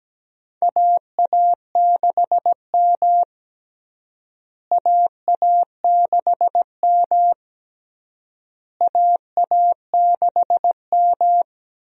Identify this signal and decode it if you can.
Morse code, heard as a keyed tone, about 17 wpm, 700 Hz. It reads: AA6M AA6M AA6M
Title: USA Call Signs - 4 Characters - Repeated 3x - 17wpm